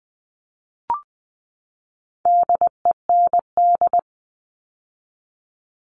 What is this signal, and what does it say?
Morse code, heard as a keyed tone, about 20 wpm, 700 Hz. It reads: DEND
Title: All Cut Numbers - 4-Digit 20wpm